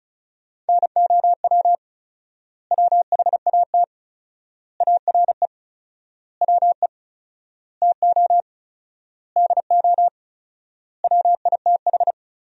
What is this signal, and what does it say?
Morse code, heard as a keyed tone, about 36 wpm, 700 Hz. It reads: NOW WHAT ARE WE TO DO WITH
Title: 4x Word-Spacing - Sentences from Top 100 Words - 35wpm